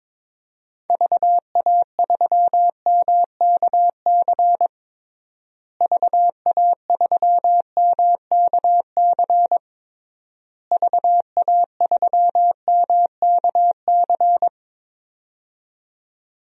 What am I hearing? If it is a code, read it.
Morse code, heard as a keyed tone, about 22 wpm, 700 Hz. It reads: VA3MKC VA3MKC VA3MKC
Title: DXCC-100 Call Signs - 6 Characters - Repeated 3x - 22wpm